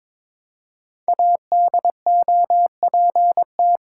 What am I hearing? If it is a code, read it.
Morse code, heard as a keyed tone, about 22 wpm, 700 Hz. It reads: ADOPT
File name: 5 Letter Words - First 3 Letters Given - 22wpm